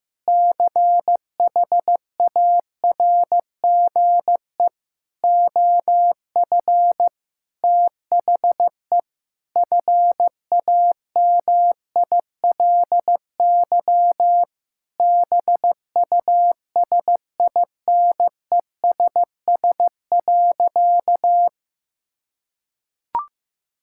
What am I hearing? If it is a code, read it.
Morse code, heard as a keyed tone, about 15 wpm, 700 Hz. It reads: CHARGE OF THE FAMILY BUSINESS.
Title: Sentences from Top 1000 Words - Review Part 6 - 15wpm